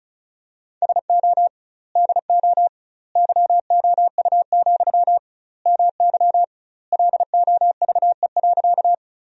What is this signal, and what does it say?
Morse code, heard as a keyed tone, about 34 wpm, 700 Hz. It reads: SO DO YOU, MY LOVE.
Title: 2x Word-Spacing - Sentences from Top 400 Words - 35wpm